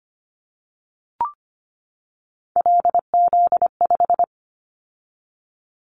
Morse code, heard as a keyed tone, about 25 wpm, 700 Hz. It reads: LZ5
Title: Callsign Prefixes for Top 25 DXCC - 25wpm